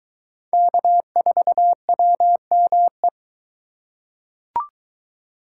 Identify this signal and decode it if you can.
Morse code, heard as a keyed tone, about 22 wpm, 700 Hz. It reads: K4WME